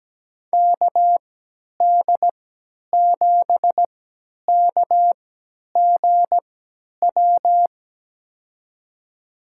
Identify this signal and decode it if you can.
Morse code, heard as a keyed tone, about 17 wpm, spaced as for 10 wpm, 700 Hz. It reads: KD7KGW
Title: USA Call Signs - 6 Characters - 3x Character Spacing - 17wpm